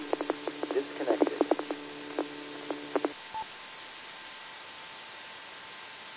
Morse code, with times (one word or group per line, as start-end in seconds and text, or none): none